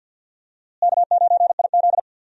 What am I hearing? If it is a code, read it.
Morse code, heard as a keyed tone, about 52 wpm, 700 Hz. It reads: K9IZ
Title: USA Call Signs - 4 Characters - Repeated 2x - 50wpm